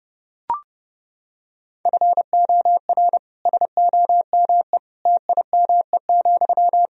Morse code, heard as a keyed tone, about 32 wpm, 700 Hz. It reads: FOR SOME TIME,
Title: Sentences from Top 1000 Words - Review Part 2 - 30wpm